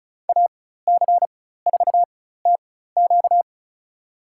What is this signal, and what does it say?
Morse code, heard as a keyed tone, about 34 wpm, spaced as for 16 wpm, 700 Hz. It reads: AC4TQ
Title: USA Call Signs - 4x Character Spacing - 35wpm